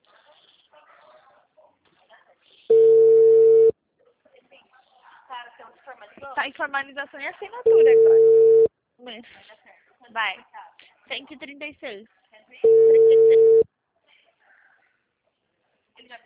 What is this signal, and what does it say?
Ringback tone (China, 450 Hz), with speech.
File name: out- -2016- - - .29990